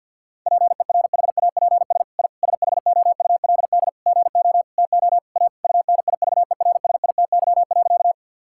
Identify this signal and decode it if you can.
Morse code, heard as a keyed tone, about 50 wpm, 700 Hz. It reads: PERHAPS I SHOULD GO TO A UNIVERSITY.
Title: Sentences from Top 1000 Words - Review Part 5 - 50wpm